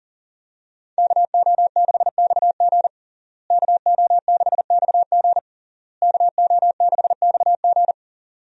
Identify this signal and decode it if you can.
Morse code, heard as a keyed tone, about 40 wpm, 700 Hz. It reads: KO6XG KO6XG KO6XG
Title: USA Call Signs - 5 Characters - Repeated 3x - 40wpm